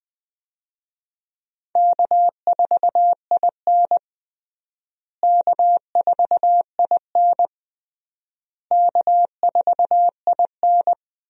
Morse code, heard as a keyed tone, about 20 wpm, 700 Hz. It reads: K4IN K4IN K4IN
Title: USA Call Signs - 4 Characters - Repeated 3x - 20wpm